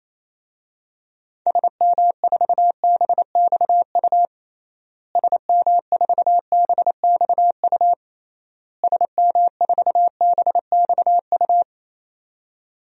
Morse code, heard as a keyed tone, about 28 wpm, 700 Hz. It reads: SM4BXU SM4BXU SM4BXU
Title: DXCC-100 Call Signs - 6 Characters - Repeated 3x - 28wpm